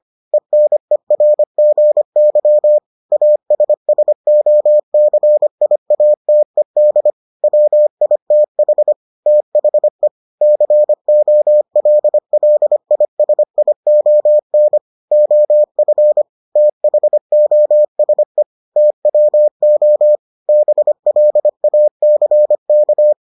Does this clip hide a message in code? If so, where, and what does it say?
Yes: Morse code from 0.3 to 23.1 s, heard as a keyed tone, about 25 wpm, 600 Hz. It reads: ENERGY ASSOCIATED WITH THE COLLISION OF THOSE TWO BLACK